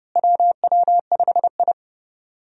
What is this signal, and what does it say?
Morse code, heard as a keyed tone, about 30 wpm, 700 Hz. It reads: WW5I
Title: USA Call Signs - 4 Characters - Repeated 3x - 30wpm